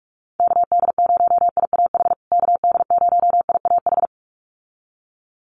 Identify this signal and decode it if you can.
Morse code, heard as a keyed tone, about 44 wpm, 700 Hz. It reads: KD0IAH KD0IAH
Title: USA Call Signs Repeated 2x 45wpm